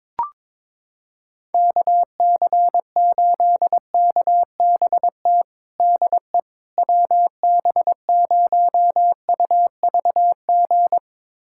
Morse code, heard as a keyed tone, about 22 wpm, 700 Hz. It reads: KC8KBT DE WB0UVG
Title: USA Callsign DE Callsign - 6 Characters - 22wpm